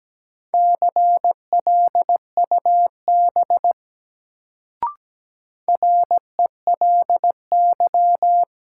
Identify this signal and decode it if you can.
Morse code, heard as a keyed tone, about 17 wpm, 700 Hz. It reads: CLUB RELY
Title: Sentences from Top 1200 Words - New Word Drill Alt - 17wpm